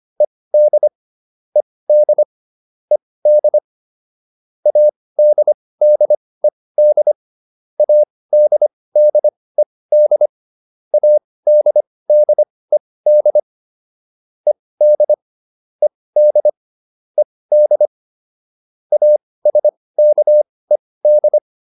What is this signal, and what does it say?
Morse code, heard as a keyed tone, about 25 wpm, spaced as for 18 wpm, 600 Hz. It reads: ED ED ED ADDED ADDED ADDED ED ED ED ASKED